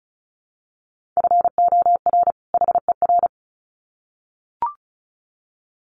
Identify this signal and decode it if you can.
Morse code, heard as a keyed tone, about 34 wpm, 700 Hz. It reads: FOR HER
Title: Sets of 2 Words 35wpm